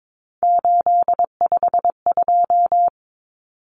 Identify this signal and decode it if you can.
Morse code, heard as a keyed tone, about 22 wpm, 700 Hz. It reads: 852
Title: Common Cut Numbers - 3-Digit 22wpm